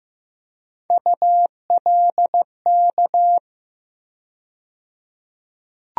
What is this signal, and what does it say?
Morse code, heard as a keyed tone, about 15 wpm, 700 Hz. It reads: ULK
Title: USA Call Sign Postfixes - 3 Letters 15wpm